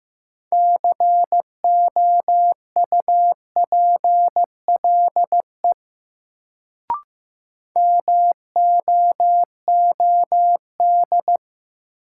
Morse code, heard as a keyed tone, about 15 wpm, 700 Hz. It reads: COUPLE MOOD
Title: Sentences from Top 1100 Words - New Word Drill - 15wpm